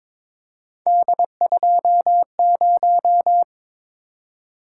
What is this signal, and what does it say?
Morse code, heard as a keyed tone, about 22 wpm, 700 Hz. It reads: D20
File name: practice / 22wpm / Three Letter-Number Combinations - 22wpm